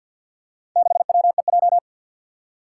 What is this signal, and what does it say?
Morse code, heard as a keyed tone, about 49 wpm, 700 Hz. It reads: 6WEJ
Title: Four Challenging Letter-Number Combinations 50wpm